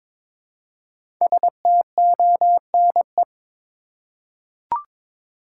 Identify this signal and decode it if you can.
Morse code, heard as a keyed tone, about 22 wpm, 700 Hz. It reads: STONE
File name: 5 Letter Words - First 3 Letters Given - 22wpm